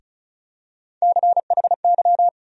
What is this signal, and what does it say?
Morse code, heard as a keyed tone, about 34 wpm, 700 Hz. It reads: CHY
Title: USA Call Sign Postfixes - 3 Letters 35wpm